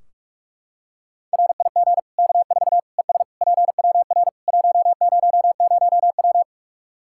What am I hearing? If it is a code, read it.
Morse code, heard as a keyed tone, about 46 wpm, 700 Hz. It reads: RIG K4 ES PWR 100W
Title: K1USN SST QSO Practice 45wpm